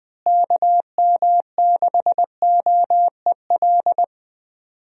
Morse code, heard as a keyed tone, about 20 wpm, 700 Hz. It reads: KM6OEL
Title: USA Call Signs - 6 Characters - Repeated 3x - 20wpm